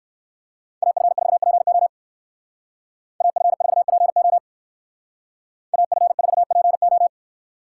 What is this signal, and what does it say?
Morse code, heard as a keyed tone, about 54 wpm, 700 Hz. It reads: AF4PO AF4PO AF4PO
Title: Speed Racing - USA Call Signs 45wpm